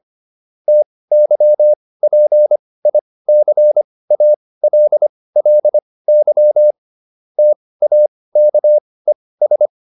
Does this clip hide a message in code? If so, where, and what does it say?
Yes: Morse code from 0.7 to 9.7 s, heard as a keyed tone, about 25 wpm, spaced as for 18 wpm, 600 Hz. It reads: TYPICALLY TAKES